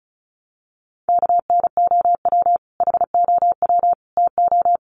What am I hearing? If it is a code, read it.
Morse code, heard as a keyed tone, about 34 wpm, 700 Hz. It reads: KNOW HOW TO
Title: Sets of 3 Words 35wpm